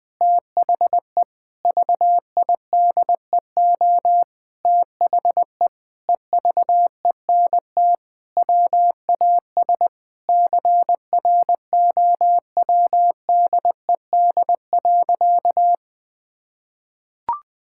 Morse code, heard as a keyed tone, about 20 wpm, 700 Hz. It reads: THE VIDEO THE EVENT WAS CROWDED.
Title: Sentences with 7-Word Phrases - 20wpm